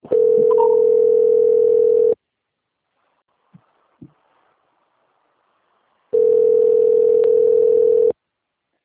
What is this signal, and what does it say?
Ringback tone (North America), 440 + 480 Hz.